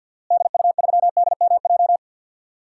Morse code, heard as a keyed tone, about 47 wpm, 700 Hz. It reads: DU3DMJ
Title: DXCC-100 Call Signs - 6 Characters - Repeated 2x - 50wpm